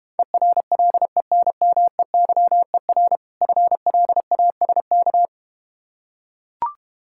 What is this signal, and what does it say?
Morse code, heard as a keyed tone, about 32 wpm, 690 Hz. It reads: ERLENMEYER FLASK